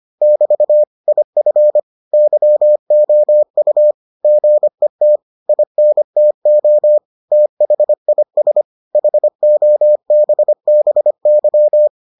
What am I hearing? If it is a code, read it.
Morse code, heard as a keyed tone, about 25 wpm, 600 Hz. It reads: <BT> IF YOU GET INTO THIS HOBBY